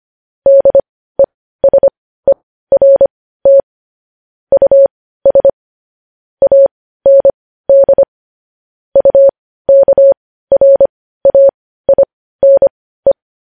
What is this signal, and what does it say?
Morse code, heard as a keyed tone, about 25 wpm, spaced as for 15 wpm, 550 Hz. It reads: DESERT US AND UKRAINE
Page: news-headlines-in-morse-code-at-15-wpm